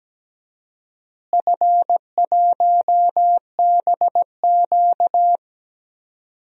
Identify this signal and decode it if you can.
Morse code, heard as a keyed tone, about 17 wpm, 700 Hz. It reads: F1BQ
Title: DXCC-100 Call Signs - 4 Characters - Repeated 2x - 17wpm